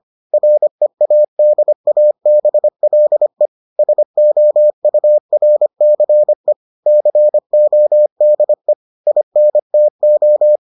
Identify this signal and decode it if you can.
Morse code, heard as a keyed tone, about 25 wpm, 600 Hz. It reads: READABLE SOURCE CODE INTO